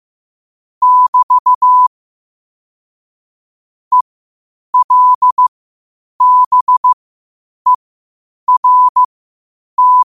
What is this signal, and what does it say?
Morse code, heard as a keyed tone, about 15 wpm, spaced as for 8 wpm, 1000 Hz. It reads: <BT> ELBERT
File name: qotdfiles / 8wpm